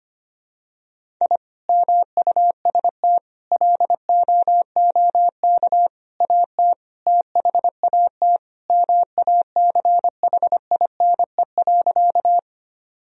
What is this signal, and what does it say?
Morse code, heard as a keyed tone, about 25 wpm, 700 Hz. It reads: I MUST LOOK AT THAT MACHINE.
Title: Sentences from Top 1000 Words - Review Part 2 - 25wpm